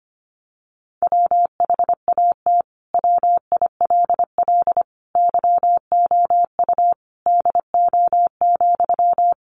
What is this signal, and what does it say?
Morse code, heard as a keyed tone, about 25 wpm, 700 Hz. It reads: WHAT WILL YOU DO,
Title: Sentences from Top 100 Words 25wpm